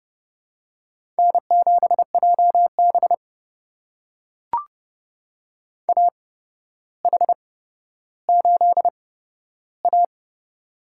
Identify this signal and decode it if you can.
Morse code, heard as a keyed tone, about 30 wpm, spaced as for 13 wpm, 700 Hz. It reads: N7JB AH8A
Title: USA Call Signs - 4 Characters - 8x Character Spacing - 30wpm